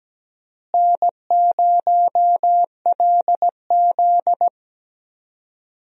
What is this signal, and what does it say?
Morse code, heard as a keyed tone, about 17 wpm, 700 Hz. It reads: N0LZ
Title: USA Call Signs - 4 Characters - Repeated 2x - 17wpm